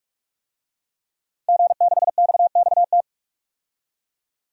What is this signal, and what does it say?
Morse code, heard as a keyed tone, about 46 wpm, 700 Hz. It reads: G6XXT
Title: DXCC-100 Call Signs - 5 Characters - 45wpm